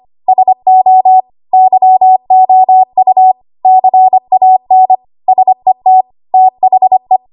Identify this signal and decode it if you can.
Morse code, heard as a keyed tone, about 25 wpm, 750 Hz. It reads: SO YOU CAN SET THE
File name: Archive / 25 WPM / 25WPM